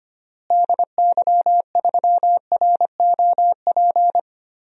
Morse code, heard as a keyed tone, about 25 wpm, 700 Hz. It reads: DY3ROP
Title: DXCC-100 Call Signs - 6 Characters - Repeated 2x - 25wpm